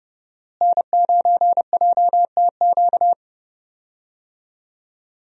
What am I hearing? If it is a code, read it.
Morse code, heard as a keyed tone, about 30 wpm, 700 Hz. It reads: N9JTQ
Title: USA Call Signs - 5 Characters - Repeated 3x - 30wpm